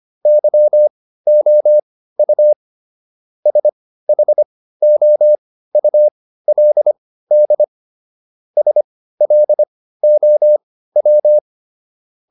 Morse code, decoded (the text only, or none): YOU SHOULD SLOW